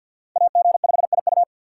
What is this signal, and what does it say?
Morse code, heard as a keyed tone, about 50 wpm, 700 Hz. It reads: AG5IU
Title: USA Call Signs - 5 Characters - 6x Character Spacing - 50wpm